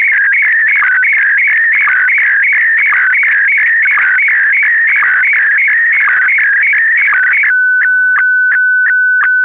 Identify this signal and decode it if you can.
SSTV (slow-scan television), Scottie DX.